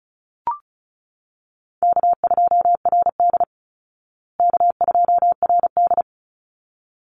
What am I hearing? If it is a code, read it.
Morse code, heard as a keyed tone, about 34 wpm, 700 Hz. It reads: K2RD K2RD
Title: Call Signs Repeated 3x - 35wpm